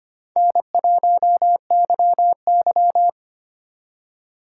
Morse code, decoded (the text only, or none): N1YY